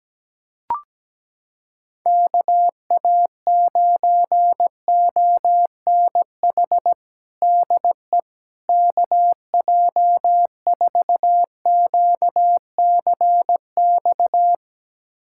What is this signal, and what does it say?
Morse code, heard as a keyed tone, about 17 wpm, 700 Hz. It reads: KA9ONH DE KJ4QCX
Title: USA Callsign DE Callsign - 6 Characters - 17wpm